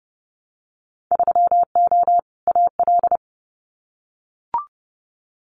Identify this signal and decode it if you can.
Morse code, heard as a keyed tone, about 30 wpm, 700 Hz. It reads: 3O AL
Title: Winter Field Day Exchanges Encore 30wpm